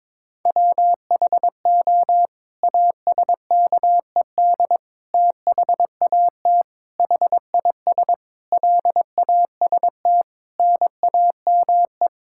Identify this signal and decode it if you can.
Morse code, heard as a keyed tone, about 22 wpm, 700 Hz. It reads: WHO ASKED THAT HIS LAST NAME
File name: Sets of 9 Words - 22wpm